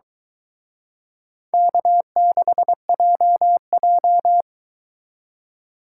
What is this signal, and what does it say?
Morse code, heard as a keyed tone, about 23 wpm, 700 Hz. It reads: K6JJ